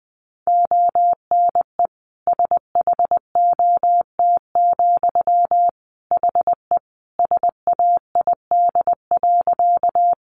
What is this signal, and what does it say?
Morse code, heard as a keyed tone, about 20 wpm, 700 Hz. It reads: ONE SHOT, HE SAID.
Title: Sentences from Top 1100 Words 20wpm